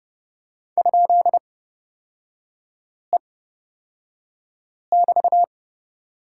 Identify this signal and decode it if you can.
Morse code, heard as a keyed tone, about 30 wpm, spaced as for 5 wpm, 700 Hz. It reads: ?E<BT>
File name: Lesson 36e - BT - Three Characters 30wpm